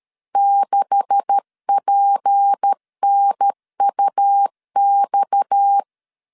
Morse code, heard as a keyed tone, about 13 wpm, 800 Hz. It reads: 6PNUX
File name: CW / DET / 1 9 1500C S A 12wpm 3800